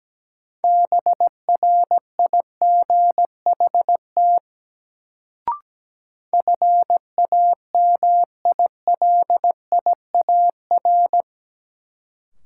Morse code, heard as a keyed tone, about 17 wpm, 700 Hz. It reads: BRIGHT FAMILIAR
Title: Sentences from Top 1000 Words - New Word Drill - 17wpm